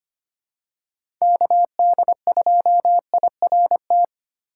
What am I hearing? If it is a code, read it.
Morse code, heard as a keyed tone, about 25 wpm, 700 Hz. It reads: KD2IRT
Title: USA Call Signs - 6 Characters - 2x Character Spacing - 25wpm